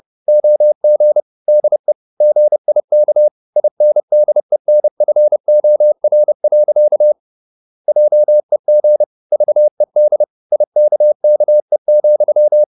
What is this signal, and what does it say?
Morse code, heard as a keyed tone, about 30 wpm, 600 Hz. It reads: OG DE GIK INDENFOR. JEG VED IKKE,
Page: Ferie 30 wpm